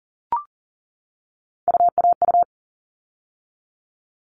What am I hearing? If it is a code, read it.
Morse code, heard as a keyed tone, about 40 wpm, 700 Hz. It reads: UAU
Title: All Cut Numbers - 3-Digit 40wpm